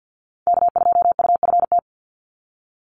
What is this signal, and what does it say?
Morse code, heard as a keyed tone, about 48 wpm, 700 Hz. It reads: K2UFT
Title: Call Signs Repeated 3x - 50wpm